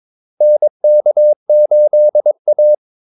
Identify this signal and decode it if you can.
Morse code, heard as a keyed tone, about 22 wpm, 600 Hz. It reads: NK8A
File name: Callsigns-22wpm